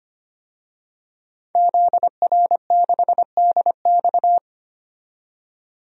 Morse code, heard as a keyed tone, about 25 wpm, 700 Hz. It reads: ZR6DX
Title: DXCC-100 Call Signs - 5 Characters - 25wpm